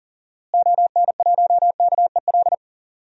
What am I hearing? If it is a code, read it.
Morse code, heard as a keyed tone, about 40 wpm, 700 Hz. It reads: ON1KEL
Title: DXCC-100 Call Signs - 6 Characters - Repeated 3x - 40wpm